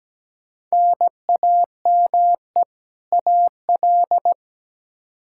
Morse code, heard as a keyed tone, about 17 wpm, 700 Hz. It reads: NAME AL
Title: K1USN SST QSO Practice 17wpm